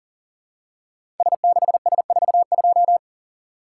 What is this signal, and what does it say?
Morse code, heard as a keyed tone, about 40 wpm, 700 Hz. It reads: S6S42
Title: Code Groups - 5 Letter:Numbers - 40wpm